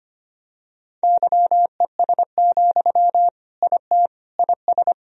Morse code, heard as a keyed tone, about 25 wpm, 700 Hz. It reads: YES, IT IS
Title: Sentences from Top 1000 Words - Review Part 1 - 25wpm